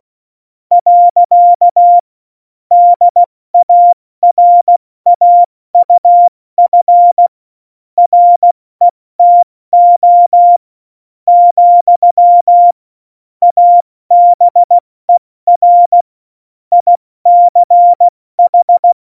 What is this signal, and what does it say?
Morse code, heard as a keyed tone, about 16 wpm, spaced as for 13 wpm, 700 Hz. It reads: . DARAUF RETO, ABER ICH
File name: Morsekurs / Witze / Witz 17 April 2018, 70 F80